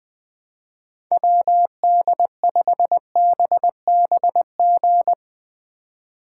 Morse code, heard as a keyed tone, about 20 wpm, 700 Hz. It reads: WD5BBG